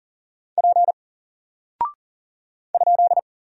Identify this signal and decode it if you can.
Morse code, heard as a keyed tone, about 40 wpm, 700 Hz. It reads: P ?
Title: Single Character (Letter:Number:Punctuation:Prosign) Alt 40wpm